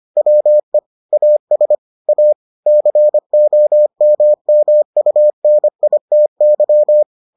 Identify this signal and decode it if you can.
Morse code, heard as a keyed tone, about 25 wpm, 600 Hz. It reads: WE AS A COMMUNITY